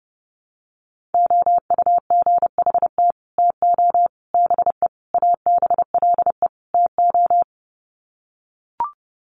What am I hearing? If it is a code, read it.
Morse code, heard as a keyed tone, about 30 wpm, 700 Hz. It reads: OUGHT TO BE ABLE TO
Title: Sets of 5 Words Encore2 30wpm